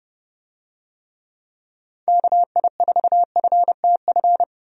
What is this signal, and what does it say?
Morse code, heard as a keyed tone, about 32 wpm, 700 Hz. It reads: KI4FTF
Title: USA Call Signs - 6 Characters - 30wpm